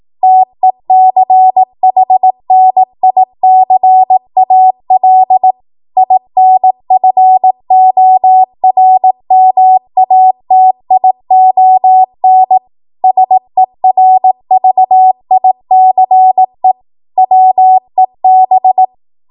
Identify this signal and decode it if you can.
Morse code, heard as a keyed tone, about 18 wpm, 750 Hz. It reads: TECHNICAL INFORMATION SERVICE WEB
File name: Archive / 18 WPM / 18WPM